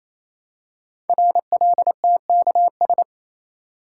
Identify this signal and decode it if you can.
Morse code, heard as a keyed tone, about 27 wpm, 700 Hz. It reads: RLTKS
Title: Code Groups - 5 Letters - 28wpm